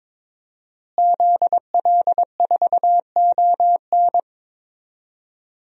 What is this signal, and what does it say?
Morse code, heard as a keyed tone, about 22 wpm, 700 Hz. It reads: ZL4ON
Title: DXCC-100 Call Signs - 5 Characters - Repeated 3x - 22wpm